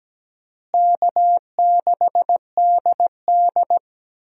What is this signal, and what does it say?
Morse code, heard as a keyed tone, about 17 wpm, 700 Hz. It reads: K6DD
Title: USA Call Signs - 4 Characters - Repeated 2x - 17wpm